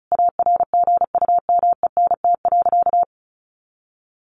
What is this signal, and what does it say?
Morse code, heard as a keyed tone, about 34 wpm, 700 Hz. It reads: ARGUMENT.
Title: A Sound Of Thunder 35wpm